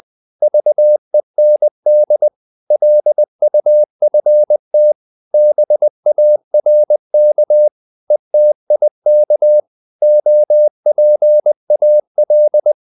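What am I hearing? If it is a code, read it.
Morse code, heard as a keyed tone, about 20 wpm, 600 Hz. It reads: VEND LUFT BARK ETIK OPAL
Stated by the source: Morse code practice of short Danish words at 20 wpm, from CW med Gnister.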